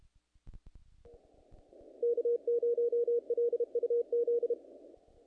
Morse code, heard as a keyed tone, about 31 wpm, 470 Hz. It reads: K0LUZ
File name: contests / naqp-cw-k5tr-op-k5pi- -2000